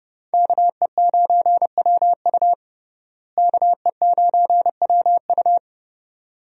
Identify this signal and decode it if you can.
Morse code, heard as a keyed tone, about 31 wpm, 700 Hz. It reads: KE9WU KE9WU
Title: USA Call Signs - 5 Characters - Repeated 2x - 30wpm